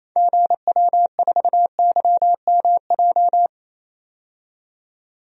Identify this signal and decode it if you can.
Morse code, heard as a keyed tone, about 28 wpm, 700 Hz. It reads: GW4YMJ
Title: DXCC-100 Call Signs - 6 Characters - 28wpm